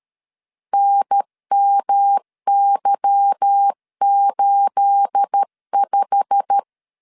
Morse code, heard as a keyed tone, about 13 wpm, 800 Hz. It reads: NMY85
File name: LEZIONI / 9 28 12wpm 2351